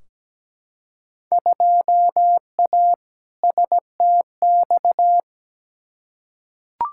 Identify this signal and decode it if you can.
Morse code, heard as a keyed tone, about 17 wpm, 700 Hz. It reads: 2A STX
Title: Field Day Exchanges - 17wpm